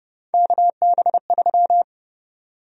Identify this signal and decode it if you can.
Morse code, heard as a keyed tone, about 31 wpm, 700 Hz. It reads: KB3
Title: USA Call Sign Prefixes - 30wpm